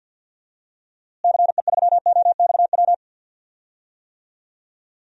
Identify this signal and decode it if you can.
Morse code, heard as a keyed tone, about 47 wpm, 700 Hz. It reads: CE3OXW
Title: DXCC-100 Call Signs - 6 Characters - Repeated 3x - 50wpm